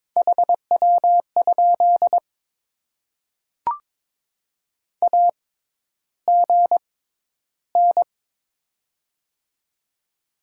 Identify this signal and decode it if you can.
Morse code, heard as a keyed tone, about 23 wpm, spaced as for 11 wpm, 700 Hz. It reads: HW? AGN
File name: Top 20 QSO Elements - 6x Character Spacing - 22wpm